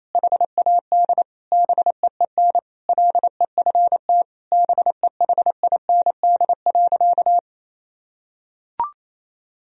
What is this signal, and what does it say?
Morse code, heard as a keyed tone, about 28 wpm, 700 Hz. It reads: HAD BEEN LEFT BEHIND.
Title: Sentences from Top 1000 Words - Review Part 1 - 28wpm